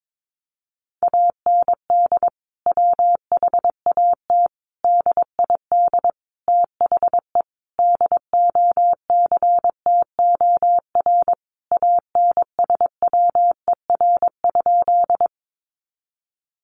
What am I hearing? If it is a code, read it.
Morse code, heard as a keyed tone, about 22 wpm, 700 Hz. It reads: AND WHAT DID THE DOCTOR ANSWER?
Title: Sentences from Top 1500 Words 22wpm